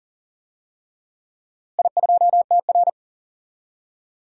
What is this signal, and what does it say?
Morse code, heard as a keyed tone, about 40 wpm, 700 Hz. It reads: I2TR
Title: DXCC-100 Call Signs - 4 Characters - Repeated 3x - 40wpm